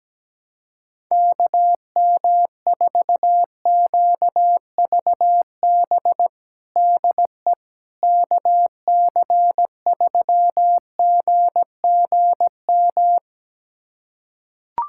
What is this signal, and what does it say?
Morse code, heard as a keyed tone, about 17 wpm, 700 Hz. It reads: KM4QVB DE KC3GGM
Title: USA Callsign DE Callsign - 6 Characters - 17wpm